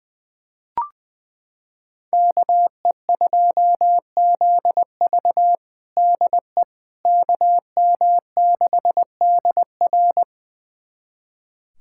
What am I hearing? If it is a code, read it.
Morse code, heard as a keyed tone, about 21 wpm, 700 Hz. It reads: KE2ZV DE KM6DR
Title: USA Callsign DE Callsign - 5 Characters - 20wpm